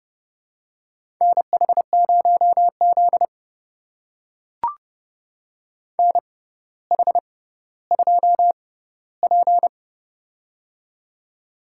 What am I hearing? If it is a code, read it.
Morse code, heard as a keyed tone, about 30 wpm, spaced as for 15 wpm, 700 Hz. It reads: NH0Z NH2P
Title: USA Call Signs - 4 Characters - 6x Character Spacing - 30wpm